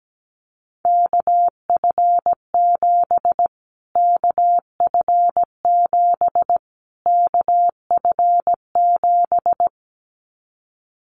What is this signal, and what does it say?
Morse code, heard as a keyed tone, about 17 wpm, 700 Hz. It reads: KF7 KF7 KF7
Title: Callsign Prefixes for Top 25 DXCC - Repeated 3x - 17wpm